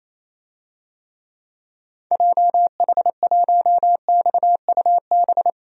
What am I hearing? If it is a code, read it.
Morse code, heard as a keyed tone, about 28 wpm, 700 Hz. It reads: JH1XUB
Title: DXCC-100 Call Signs - 6 Characters - 28wpm